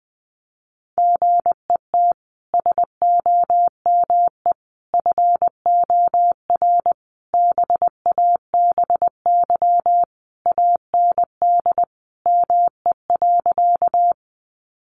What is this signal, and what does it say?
Morse code, heard as a keyed tone, about 20 wpm, 700 Hz. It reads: GET SOME FOR BABY AND ME.
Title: Sentences from Top 1300 Words 20wpm